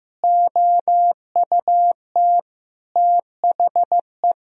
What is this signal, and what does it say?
Morse code, heard as a keyed tone, about 15 wpm, 700 Hz. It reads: OUT THE
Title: Sentences from Top 1000 Words - Review Part 3 - 15wpm